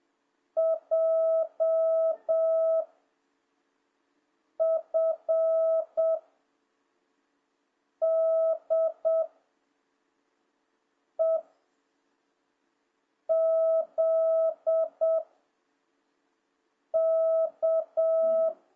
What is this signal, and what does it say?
Morse code, heard as a keyed tone, about 7 wpm, 650 Hz. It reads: J F D E Z K